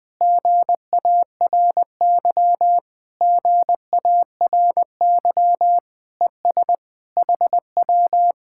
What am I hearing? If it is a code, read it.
Morse code, heard as a keyed tone, about 21 wpm, 700 Hz. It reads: GARY GARY ES HW
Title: Speed-Racing Common QSO Phrases - 20wpm